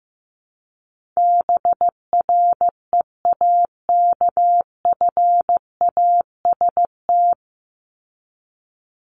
Morse code, heard as a keyed tone, about 15 wpm, 700 Hz. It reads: BREAKFAST
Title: Top 1251-1500 Words 15wpm